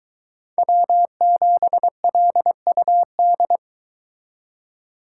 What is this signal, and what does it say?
Morse code, heard as a keyed tone, about 22 wpm, 700 Hz. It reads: W7LUD